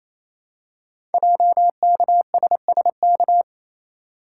Morse code, heard as a keyed tone, about 29 wpm, 700 Hz. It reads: JKSSK
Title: Code Groups - 5 Letter:Numbers - 28wpm